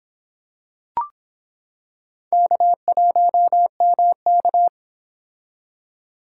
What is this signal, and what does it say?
Morse code, heard as a keyed tone, about 25 wpm, 700 Hz. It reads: K1MK